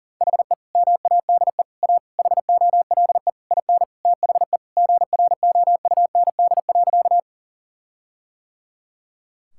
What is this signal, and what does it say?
Morse code, heard as a keyed tone, about 40 wpm, 700 Hz. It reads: HE MADE A HOLE IN THE GROUND.
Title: Sentences from Top 1600 Words 40wpm